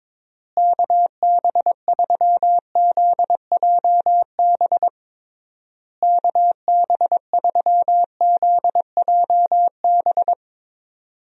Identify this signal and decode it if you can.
Morse code, heard as a keyed tone, about 21 wpm, 700 Hz. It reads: KB3ZJB KB3ZJB
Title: USA Call Signs - 6 Characters - Repeated 3x - 22wpm